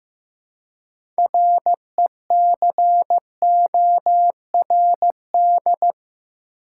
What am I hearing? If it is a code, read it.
Morse code, heard as a keyed tone, about 15 wpm, 700 Hz. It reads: RECORD
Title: Six Letter Words - 15wpm